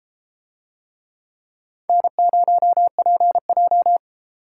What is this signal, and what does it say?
Morse code, heard as a keyed tone, about 31 wpm, 700 Hz. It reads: N0PJ